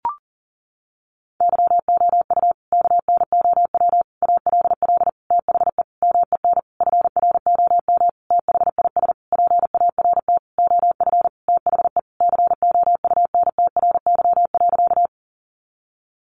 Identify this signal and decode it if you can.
Morse code, heard as a keyed tone, about 40 wpm, 700 Hz. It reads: YOU KNOW ALL THE MEN FROM THIS PART OF THE COUNTRY.
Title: Sentences from Top 300 Words 40wpm